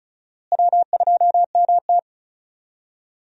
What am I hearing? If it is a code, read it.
Morse code, heard as a keyed tone, about 36 wpm, 700 Hz. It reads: W2MT
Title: USA Call Signs - 4 Characters - Repeated 2x - 35wpm